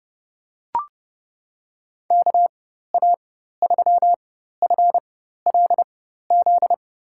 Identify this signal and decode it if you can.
Morse code, heard as a keyed tone, about 30 wpm, spaced as for 14 wpm, 700 Hz. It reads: KA3FLZ
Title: USA Call Signs - 4x Character Spacing - 30wpm